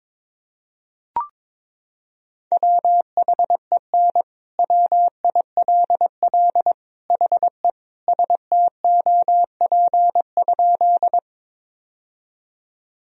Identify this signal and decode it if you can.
Morse code, heard as a keyed tone, about 22 wpm, 700 Hz. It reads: WHEN WILL HE STOP?
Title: Sentences from Top 1000 Words - Review Part 1 - 22wpm